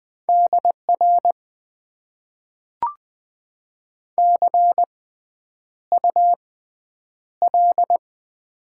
Morse code, heard as a keyed tone, about 20 wpm, spaced as for 9 wpm, 700 Hz. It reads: DR CUL
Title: Top 90 QSO Elements - 6x Character Spacing - 20wpm